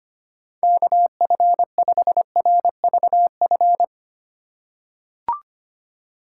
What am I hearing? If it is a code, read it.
Morse code, heard as a keyed tone, about 25 wpm, 700 Hz. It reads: KF5RVF
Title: USA Call Signs - 6 Characters - 3x Character Spacing - 25wpm